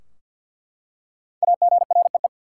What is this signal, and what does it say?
Morse code, heard as a keyed tone, about 47 wpm, 700 Hz. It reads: AGREE
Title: Five Letter Words - 50wpm